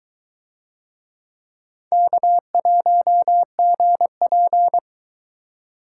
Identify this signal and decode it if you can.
Morse code, heard as a keyed tone, about 22 wpm, 700 Hz. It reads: K1GP